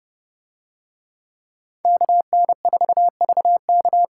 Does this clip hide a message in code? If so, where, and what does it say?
Yes: Morse code from 1.9 to 4.0 s, heard as a keyed tone, about 29 wpm, 700 Hz. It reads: KN4VK